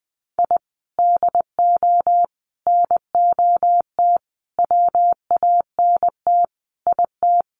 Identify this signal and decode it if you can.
Morse code, heard as a keyed tone, about 20 wpm, 700 Hz. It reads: I DO NOT WANT IT
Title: Sets of 5 Words Encore2 20wpm